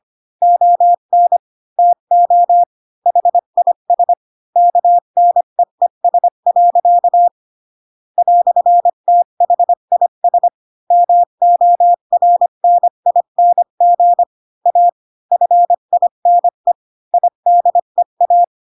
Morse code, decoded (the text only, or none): ON TO HIS KNEES. "THIS MORNING A FINE IDEA